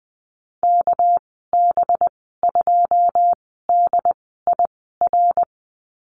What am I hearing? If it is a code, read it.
Morse code, heard as a keyed tone, about 20 wpm, spaced as for 14 wpm, 700 Hz. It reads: KB2DIR
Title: 245 Lesson 39g - Letter z - Call Signs 20wpm